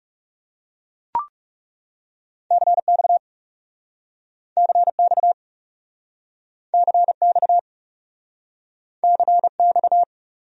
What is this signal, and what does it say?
Morse code, heard as a keyed tone, about 40 wpm, 700 Hz. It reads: CX CX CX CX